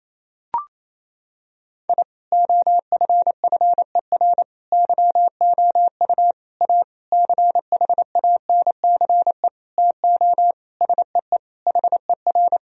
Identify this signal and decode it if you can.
Morse code, heard as a keyed tone, about 28 wpm, 700 Hz. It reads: I OFFER YOU A CHANCE TO SEE HER
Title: Sentences from Top 1000 Words - Review Part 4 - 28wpm